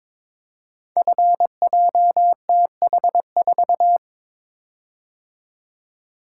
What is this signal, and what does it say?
Morse code, heard as a keyed tone, about 22 wpm, 700 Hz. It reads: FJTH4
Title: Code Groups - 5 Letter:Numbers - 22wpm